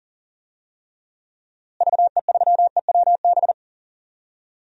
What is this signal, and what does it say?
Morse code, heard as a keyed tone, about 40 wpm, 700 Hz. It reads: VE3EWB
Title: DXCC-100 Call Signs - 6 Characters - Repeated 2x - 40wpm